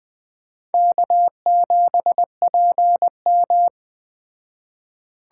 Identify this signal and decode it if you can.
Morse code, heard as a keyed tone, about 20 wpm, 700 Hz. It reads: K7PM
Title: Speed Racing - USA 1x2 Call Signs 17wpm